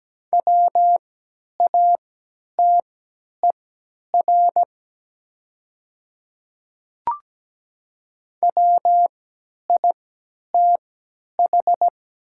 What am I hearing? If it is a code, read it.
Morse code, heard as a keyed tone, about 17 wpm, spaced as for 10 wpm, 700 Hz. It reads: WATER WITH
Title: 3x Character Spacing - Top 100 Words - 17wpm